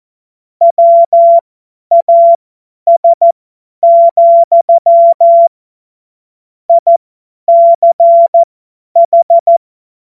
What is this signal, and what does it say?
Morse code, heard as a keyed tone, about 14 wpm, spaced as for 10 wpm, 680 Hz. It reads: WAS, ICH